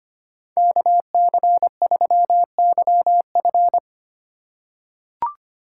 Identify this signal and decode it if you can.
Morse code, heard as a keyed tone, about 25 wpm, 700 Hz. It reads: KC3YF
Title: USA Call Signs - 5 Characters - 3x Character Spacing - 25wpm